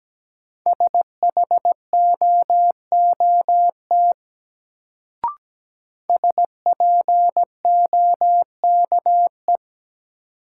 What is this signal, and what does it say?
Morse code, heard as a keyed tone, about 17 wpm, 700 Hz. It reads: SHOOT SPOKE
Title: Sentences from Top 1200 Words - New Word Drill - 17wpm